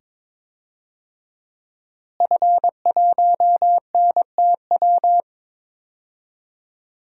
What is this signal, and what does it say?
Morse code, heard as a keyed tone, about 22 wpm, 700 Hz. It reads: F1NTW
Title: DXCC-100 Call Signs - 5 Characters - Repeated 2x - 22wpm